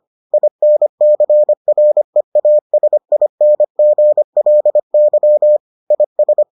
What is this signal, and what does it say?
Morse code, heard as a keyed tone, about 25 wpm, 600 Hz. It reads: INCREASINGLY IS